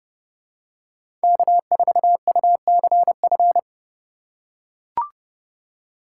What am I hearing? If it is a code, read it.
Morse code, heard as a keyed tone, about 30 wpm, 700 Hz. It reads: K4UCF